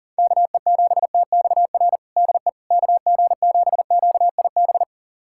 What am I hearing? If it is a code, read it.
Morse code, heard as a keyed tone, about 40 wpm, 700 Hz. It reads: KE7TXR DE KG7QIB
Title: USA Callsign DE Callsign - 6 Characters - 40wpm